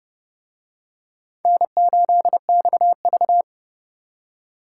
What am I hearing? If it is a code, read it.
Morse code, heard as a keyed tone, about 30 wpm, 700 Hz. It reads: N8XV